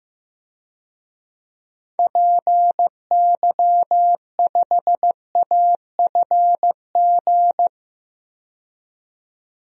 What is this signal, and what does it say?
Morse code, heard as a keyed tone, about 15 wpm, 700 Hz. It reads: PY5AFG
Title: DXCC-100 Call Signs - 6 Characters - 15wpm